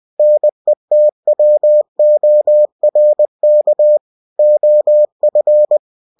Morse code, heard as a keyed tone, about 20 wpm, 600 Hz. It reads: NETWORK OF
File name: cw-books / winnie-the-pooh / winnie-20wpm-page-0122